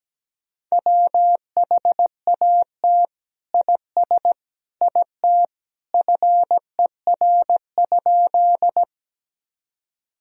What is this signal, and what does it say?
Morse code, heard as a keyed tone, about 17 wpm, 700 Hz. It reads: WHAT IS IT FER?
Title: Sentences from Top 100 Words and Abbreviations 17wpm